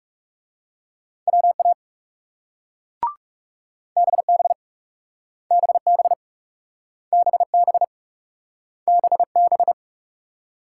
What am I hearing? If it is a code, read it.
Morse code, heard as a keyed tone, about 39 wpm, 700 Hz. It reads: WA BB BB BB BB